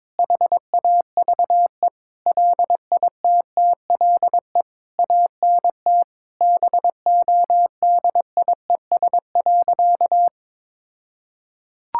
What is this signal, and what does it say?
Morse code, heard as a keyed tone, about 22 wpm, 700 Hz. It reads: HAVE LITTLE ANT BODIES.
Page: morse-code-ninja-practice-22wpm